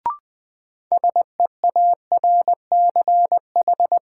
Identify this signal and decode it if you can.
Morse code, heard as a keyed tone, about 21 wpm, 700 Hz. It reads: SEARCH
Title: Sentences from Top 800 Words - New Word Drill - 20wpm